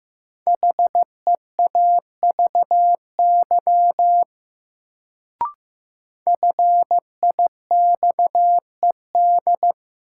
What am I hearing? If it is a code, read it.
Morse code, heard as a keyed tone, about 15 wpm, 700 Hz. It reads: HEAVY FIXED
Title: Sentences from Top 1100 Words - New Word Drill Alt - 15wpm